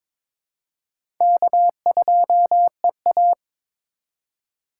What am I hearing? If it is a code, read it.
Morse code, heard as a keyed tone, about 21 wpm, 700 Hz. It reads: K2EA
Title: USA Call Signs - 4 Characters - 2x Character Spacing - 22wpm